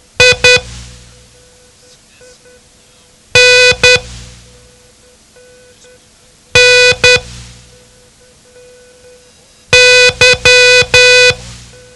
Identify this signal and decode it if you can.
Morse code, heard as a keyed tone, about 10 wpm, 500 Hz. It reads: I N N Y
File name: ham radio / CodeQuick / Code Quick 07